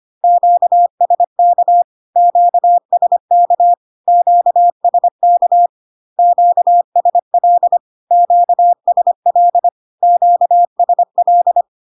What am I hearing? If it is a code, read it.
Morse code, heard as a keyed tone, about 25 wpm, 700 Hz. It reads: QSK QSK QSK QSL QSL QSL
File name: Q-codes most common repeated 3 times 25 25wpm